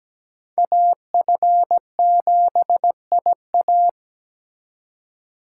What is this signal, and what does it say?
Morse code, heard as a keyed tone, about 20 wpm, 700 Hz. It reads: AF7IA
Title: Speed Racing - USA Call Signs 17wpm